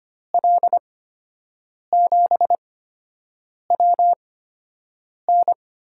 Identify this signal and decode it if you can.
Morse code, heard as a keyed tone, about 25 wpm, spaced as for 7 wpm, 700 Hz. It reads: L7WN
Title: USA Call Signs - 8x Character Spacing - 25wpm